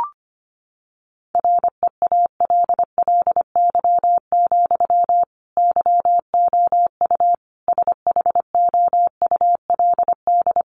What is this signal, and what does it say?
Morse code, heard as a keyed tone, about 25 wpm, 700 Hz. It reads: REALLY, YOU SHOULD
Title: Sentences from Top 1400 Words 25wpm